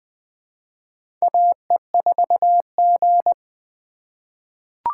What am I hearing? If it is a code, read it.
Morse code, heard as a keyed tone, about 20 wpm, 700 Hz. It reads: AE4G
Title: USA Call Signs - 4 Characters - 3x Character Spacing - 20wpm